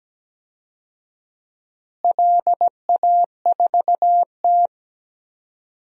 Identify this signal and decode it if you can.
Morse code, heard as a keyed tone, about 17 wpm, 700 Hz. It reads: LA4T
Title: DXCC-100 Call Signs - 4 Characters - Repeated 3x - 17wpm